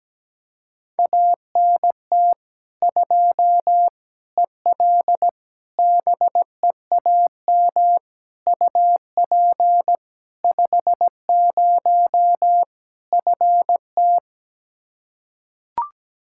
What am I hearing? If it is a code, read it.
Morse code, heard as a keyed tone, about 17 wpm, 700 Hz. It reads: ANT 2 EL BEAM UP 50 FT
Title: Practice QSO - Exchange 2 - Info - 17wpm